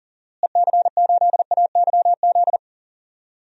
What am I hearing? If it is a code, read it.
Morse code, heard as a keyed tone, about 39 wpm, 700 Hz. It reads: EC8AYZ
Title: DXCC-100 Call Signs - 6 Characters - Repeated 3x - 40wpm